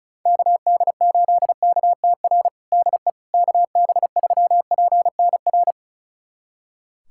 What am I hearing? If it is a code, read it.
Morse code, heard as a keyed tone, about 34 wpm, 700 Hz. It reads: KD8KTR DE KB3PNR
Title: USA Callsign DE Callsign - 6 Characters - 35wpm